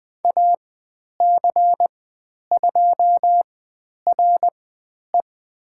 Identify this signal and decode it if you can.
Morse code, heard as a keyed tone, about 20 wpm, spaced as for 10 wpm, 700 Hz. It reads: AC2RE
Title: 102 Lesson 17g - Letter u - Call Signs 20wpm